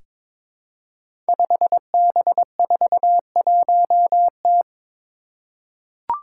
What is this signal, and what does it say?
Morse code, heard as a keyed tone, about 23 wpm, 700 Hz. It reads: 5B41T
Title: Five Challenging Letter-Number Combinations 22wpm